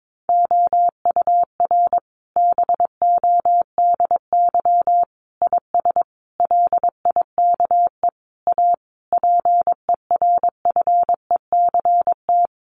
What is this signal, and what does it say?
Morse code, heard as a keyed tone, about 22 wpm, 700 Hz. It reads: OUR BODY IS LIKE A PERFECT
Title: Sentences from Top 1000 Words 22wpm